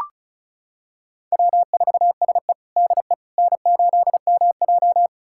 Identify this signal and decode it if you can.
Morse code, heard as a keyed tone, about 36 wpm, 700 Hz. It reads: W4SE DE N8MJ
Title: USA Callsign DE Callsign - 4 Characters - 35wpm